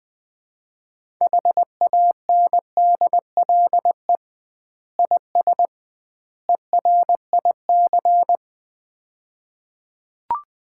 Morse code, heard as a keyed tone, about 20 wpm, 700 Hz. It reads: HANDLE IS ERIC
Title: Common QSO Phrases - 2x Word-Spacing - 20wpm